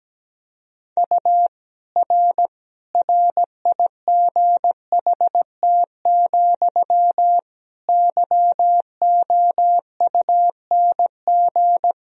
Morse code, heard as a keyed tone, about 17 wpm, 700 Hz. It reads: U R RIGHT, YOUNG
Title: Sentences from Top 400 Words and Abbreviations 17wpm